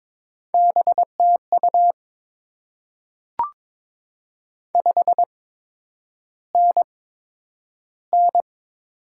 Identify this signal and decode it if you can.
Morse code, heard as a keyed tone, about 22 wpm, spaced as for 9 wpm, 700 Hz. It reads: BTU 5NN
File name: Top 20 QSO Elements - 8x Character Spacing - 22wpm